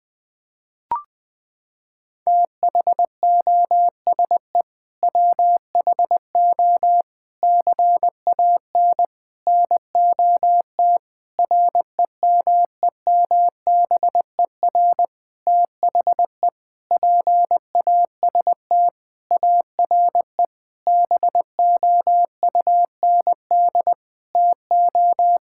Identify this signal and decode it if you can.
Morse code, heard as a keyed tone, about 20 wpm, 700 Hz. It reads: THOSE WHO CAN NOT REMEMBER THE PAST ARE BOUND TO
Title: Sentences with 8-Word Phrases - 20wpm